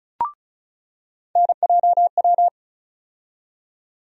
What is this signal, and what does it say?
Morse code, heard as a keyed tone, about 34 wpm, 700 Hz. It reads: NJW
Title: USA Call Sign Postfixes - 3 Letters 35wpm